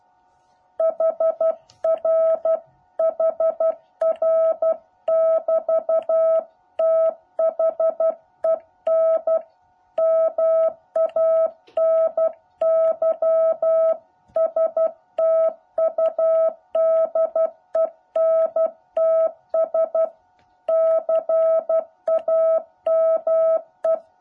Morse code, heard as a keyed tone, about 12 wpm, 650 Hz. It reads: HR HR<BT>THEN MANY STUDENTS CAME
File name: jh8cbh / 2021cw